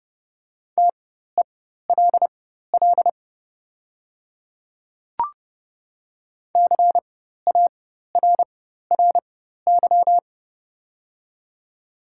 Morse code, heard as a keyed tone, about 30 wpm, spaced as for 14 wpm, 700 Hz. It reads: TELL CARRY
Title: 4x Character Spacing - Top 300 Words - 30wpm